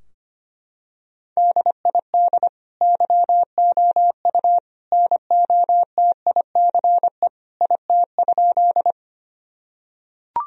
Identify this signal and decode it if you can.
Morse code, heard as a keyed tone, about 25 wpm, 700 Hz. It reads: DID YOU NOTICE IT?
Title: Sentences from Top 1000 Words - Review Part 2 - 25wpm